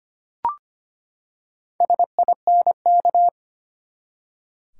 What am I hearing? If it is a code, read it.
Morse code, heard as a keyed tone, about 25 wpm, 700 Hz. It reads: SINK
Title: Four Letter Words - 25wpm